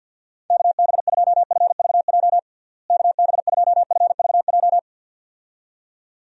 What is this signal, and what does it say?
Morse code, heard as a keyed tone, about 51 wpm, 700 Hz. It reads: KB2FVJ KB2FVJ
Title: USA Call Signs - 6 Characters - Repeated 2x - 50wpm